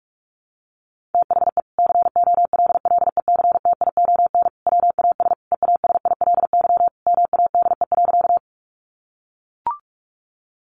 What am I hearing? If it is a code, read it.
Morse code, heard as a keyed tone, about 44 wpm, 700 Hz. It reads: THE COLLECTION WAS EASILY MADE.
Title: Sentences from Top 1200 Words 45wpm